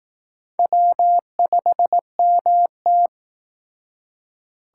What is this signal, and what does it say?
Morse code, heard as a keyed tone, about 19 wpm, 700 Hz. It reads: W5MT
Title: Speed Racing - USA Call Signs 15wpm